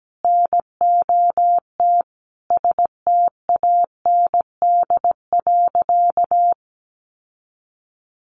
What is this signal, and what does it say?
Morse code, heard as a keyed tone, about 17 wpm, 700 Hz. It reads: NOT STAND.
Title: Sentences from Top 1300 Words 17wpm